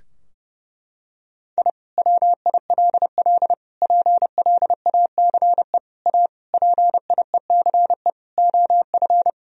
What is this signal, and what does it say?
Morse code, heard as a keyed tone, about 30 wpm, 700 Hz. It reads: I WILL PLACE A PIECE OF
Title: Sentences from Top 1000 Words - Review Part 2 - 30wpm